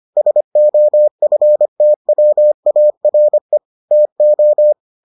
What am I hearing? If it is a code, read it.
Morse code, heard as a keyed tone, about 25 wpm, 600 Hz. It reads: SOFTWARE TO